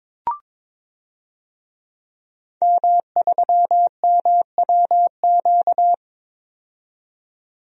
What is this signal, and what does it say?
Morse code, heard as a keyed tone, about 22 wpm, 700 Hz. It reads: M3MWQ
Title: DXCC-100 Call Signs - 5 Characters - 22wpm